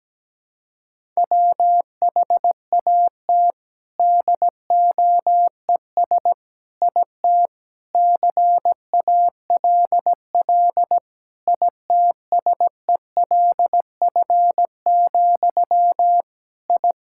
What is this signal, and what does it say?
Morse code, heard as a keyed tone, about 17 wpm, 700 Hz. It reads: WHAT DOES IT CALL ITSELF, I
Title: Sentences from Top 1000 Words - Review Part 5 - 17wpm